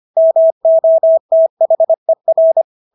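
Morse code, heard as a keyed tone, about 25 wpm, 650 Hz. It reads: MOTHER
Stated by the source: Morse practice of English words at 25 wpm from CW med Gnister.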